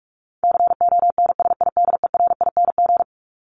Morse code, heard as a keyed tone, about 44 wpm, 700 Hz. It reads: CONSIDERING
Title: ING Suffix - Top 100 Words with Base Words 45wpm